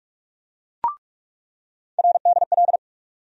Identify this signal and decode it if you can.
Morse code, heard as a keyed tone, about 46 wpm, 700 Hz. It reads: RDL
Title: USA Call Sign Postfixes - 3 Letters 45wpm